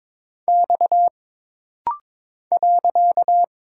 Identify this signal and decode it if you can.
Morse code, heard as a keyed tone, about 22 wpm, 700 Hz. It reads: X .
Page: This rapid-fire timing is geared to push your ICR (Instant Character Recognition) skills.